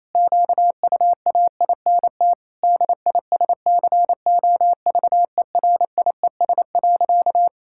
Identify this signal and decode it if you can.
Morse code, heard as a keyed tone, about 28 wpm, 700 Hz. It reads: QUAINT DISCOVERIES.
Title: Pangram Challenge v2 - 28wpm